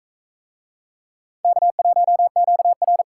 Morse code, heard as a keyed tone, about 42 wpm, 700 Hz. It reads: K1QR